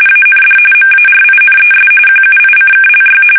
RTTY, about 45.45 baud. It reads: AND MARINES QUICKLY
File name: fsk850 / battleshipnj